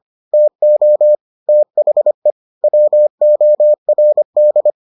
Morse code, heard as a keyed tone, about 25 wpm, 600 Hz. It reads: TO THE WORD